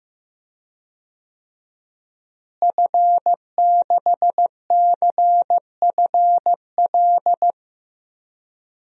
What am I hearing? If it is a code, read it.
Morse code, heard as a keyed tone, about 15 wpm, 700 Hz. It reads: F6CFL
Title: DXCC-100 Call Signs - 5 Characters - Repeated 2x - 15wpm